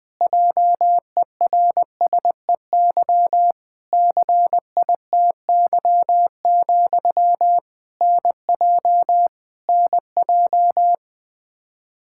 Morse code, heard as a keyed tone, about 20 wpm, 700 Hz. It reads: JERSEY CITY, NJ NJ
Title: Practice QSO - Exchange 1 - Beginning - 20wpm